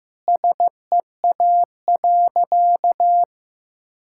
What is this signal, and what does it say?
Morse code, heard as a keyed tone, about 16 wpm, 700 Hz. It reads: SEA.
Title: Sentences from Top 1000 Words - Review Part 6 - 15wpm